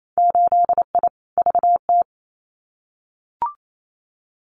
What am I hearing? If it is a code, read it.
Morse code, heard as a keyed tone, about 28 wpm, 700 Hz. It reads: 8I VT
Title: Winter Field Day Exchanges Encore 28wpm